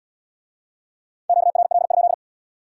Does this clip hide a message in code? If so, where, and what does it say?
Yes: Morse code from 1.3 to 2.1 s, heard as a keyed tone, about 80 wpm, 700 Hz. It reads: QRL?